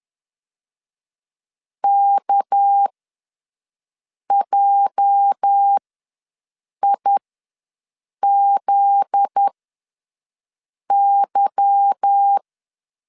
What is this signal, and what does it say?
Morse code, heard as a keyed tone, about 10 wpm, spaced as for 5 wpm, 800 Hz. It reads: KJIZY
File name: LEZIONI / 8 YJ45 / 8 YJ ws 10wpm 4043